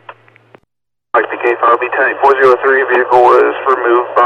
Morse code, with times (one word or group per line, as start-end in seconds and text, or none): none